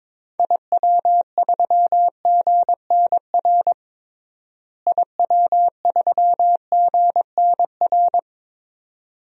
Morse code, heard as a keyed tone, about 22 wpm, 700 Hz. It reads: IW3GNR IW3GNR
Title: DXCC-100 Call Signs - 6 Characters - Repeated 2x - 22wpm